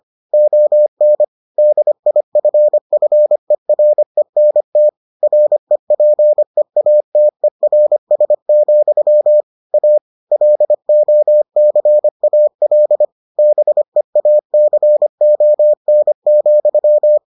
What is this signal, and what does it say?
Morse code, heard as a keyed tone, about 25 wpm, 600 Hz. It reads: ON DIFFERENT REPEATERS, A LOCAL BEACON,